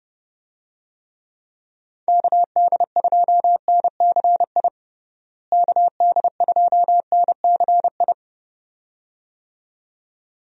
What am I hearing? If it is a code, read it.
Morse code, heard as a keyed tone, about 32 wpm, 700 Hz. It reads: KD2NCI KD2NCI
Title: USA Call Signs - 6 Characters - Repeated 2x - 30wpm